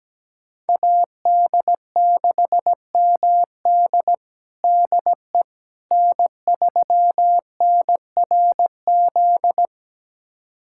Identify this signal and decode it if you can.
Morse code, heard as a keyed tone, about 17 wpm, 700 Hz. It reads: AD6MD DE N3NRZ
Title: USA Callsign DE Callsign - 5 Characters - 17wpm